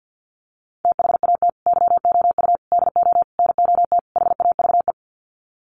Morse code, heard as a keyed tone, about 50 wpm, 700 Hz. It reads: THAT YOU DO NOT HAVE
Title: Sets of 5 Words Encore 50wpm